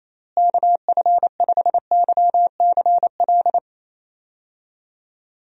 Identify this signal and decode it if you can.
Morse code, heard as a keyed tone, about 28 wpm, 700 Hz. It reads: KF5YCL
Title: USA Call Signs - 6 Characters - Repeated 3x - 28wpm